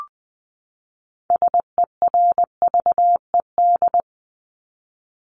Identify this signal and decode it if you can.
Morse code, heard as a keyed tone, about 20 wpm, 700 Hz. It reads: SERVED
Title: ED Suffix - Top 100 Words with Base Words 20wpm